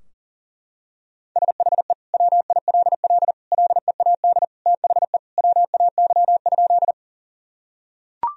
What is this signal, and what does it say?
Morse code, heard as a keyed tone, about 40 wpm, 700 Hz. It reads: SHE WILL LEAD THE WAY?
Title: Sentences from Top 1000 Words - Review Part 2 - 40wpm